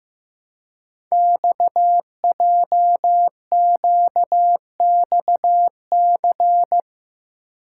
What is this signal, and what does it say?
Morse code, heard as a keyed tone, about 15 wpm, 700 Hz. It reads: XJQXC
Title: Code Groups - 5 Letters - 15wpm